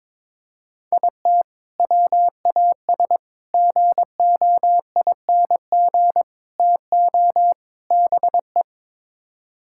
Morse code, heard as a keyed tone, about 22 wpm, 700 Hz. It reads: IT WAS GOING TO BE
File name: practice / 22wpm / Build A Phrase - 5-Word Phrases - 22wpm